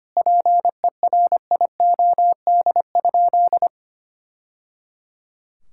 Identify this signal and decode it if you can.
Morse code, heard as a keyed tone, about 25 wpm, 700 Hz. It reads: PERIOD?
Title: Sentences from Top 1000 Words - Review Part 5 - 25wpm